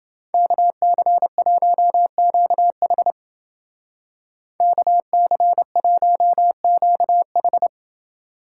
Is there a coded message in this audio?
Yes: Morse code from 0.3 to 7.7 s, heard as a keyed tone, about 28 wpm, 700 Hz. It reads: KC1QH KC1QH